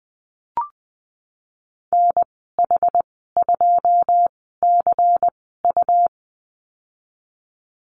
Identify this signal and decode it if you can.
Morse code, heard as a keyed tone, about 20 wpm, 700 Hz. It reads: N H 2 C U
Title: 154 Lesson 25g - Letter p - Call Signs 20wpm